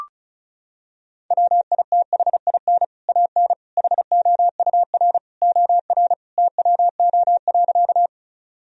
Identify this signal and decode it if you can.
Morse code, heard as a keyed tone, about 36 wpm, 700 Hz. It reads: WITHIN AN HOUR OR TWO.
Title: Sentences from Top 1000 Words - Review Part 4 - 35wpm